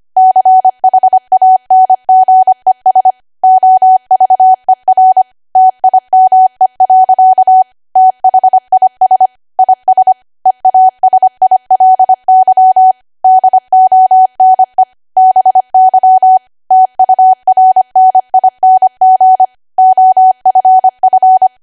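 Morse code, heard as a keyed tone, about 25 wpm, 750 Hz. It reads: CHANGES OVER TIME. THIS IS EASILY DONE BY TURNING OFF
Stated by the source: ARRL 25 WPM code practice archive.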